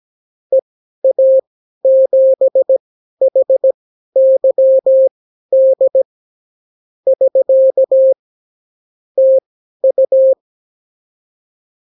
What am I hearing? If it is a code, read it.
Morse code, heard as a keyed tone, about 17 wpm, spaced as for 12 wpm, 530 Hz. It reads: EA7HYD <SK> TU
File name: CW QSO libroCW4 18-8ppm